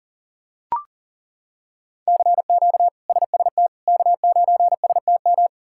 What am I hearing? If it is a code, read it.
Morse code, heard as a keyed tone, about 40 wpm, 700 Hz. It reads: CQ SST K9STM
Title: K1USN SST QSO Practice - 40wpm